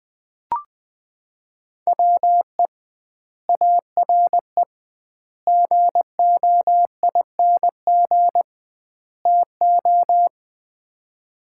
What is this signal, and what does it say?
Morse code, heard as a keyed tone, about 21 wpm, 700 Hz. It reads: WE ARE GOING TO
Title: 2x Word-Spacing - Sets of 4 Words - 20wpm